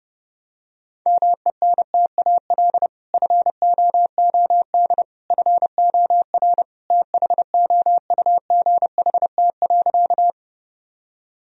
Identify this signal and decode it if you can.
Morse code, heard as a keyed tone, about 30 wpm, 700 Hz. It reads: MENTAL FOOD FOR THOUGHT.
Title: Sentences from Top 1000 Words - Review Part 4 - 30wpm